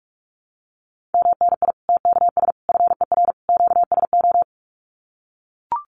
Morse code, heard as a keyed tone, about 44 wpm, 700 Hz. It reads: MNI TKS FER QSO
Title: Common QSO Phrases 45wpm